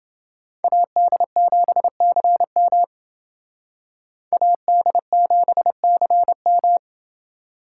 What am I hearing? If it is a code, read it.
Morse code, heard as a keyed tone, about 29 wpm, 700 Hz. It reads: AD7CM AD7CM